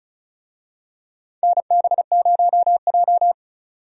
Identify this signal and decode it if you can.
Morse code, heard as a keyed tone, about 34 wpm, 700 Hz. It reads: NB0J
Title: USA Call Signs - 4 Characters - 35wpm